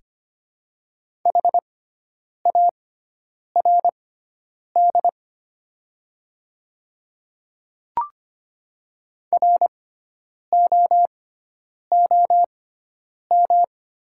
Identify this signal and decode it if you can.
Morse code, heard as a keyed tone, about 25 wpm, spaced as for 9 wpm, 700 Hz. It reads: HARD ROOM
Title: 6x Character Spacing - Top 300 Words - 25wpm